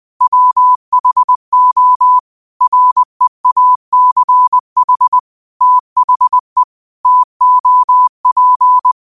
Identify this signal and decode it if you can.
Morse code, heard as a keyed tone, about 20 wpm, 1000 Hz. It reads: WHO REACH THE TOP